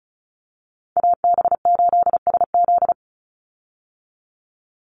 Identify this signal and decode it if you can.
Morse code, heard as a keyed tone, about 34 wpm, 700 Hz. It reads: AB8SZ
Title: USA Call Signs 35wpm